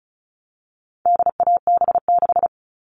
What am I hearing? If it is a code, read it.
Morse code, heard as a keyed tone, about 34 wpm, 700 Hz. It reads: DAB6
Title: All Cut Numbers - 4-Digit 35wpm